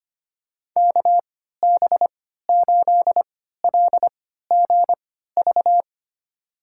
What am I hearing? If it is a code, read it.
Morse code, heard as a keyed tone, about 25 wpm, spaced as for 14 wpm, 700 Hz. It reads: KB8LGV
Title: USA Call Signs - 6 Characters - 3x Character Spacing - 25wpm